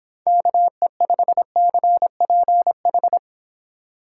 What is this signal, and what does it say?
Morse code, heard as a keyed tone, about 27 wpm, 700 Hz. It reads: KE5CPH
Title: Speed Racing - USA 2x3 Call Signs 22wpm